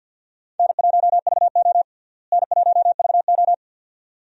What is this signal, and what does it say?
Morse code, heard as a keyed tone, about 46 wpm, 700 Hz. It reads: N1VO N1VO
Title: USA Call Signs - 4 Characters - Repeated 2x - 50wpm